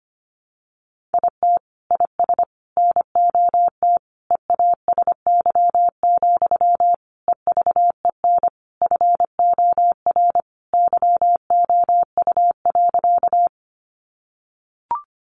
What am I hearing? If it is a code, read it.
Morse code, heard as a keyed tone, about 25 wpm, 700 Hz. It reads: IT IS NOT EASY, EVEN FOR YOU.
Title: Sentences from Top 700 Words 25wpm